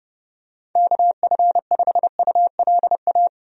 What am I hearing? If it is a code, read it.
Morse code, heard as a keyed tone, about 32 wpm, 700 Hz. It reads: KF5ULA
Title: USA Call Signs - 6 Characters - 30wpm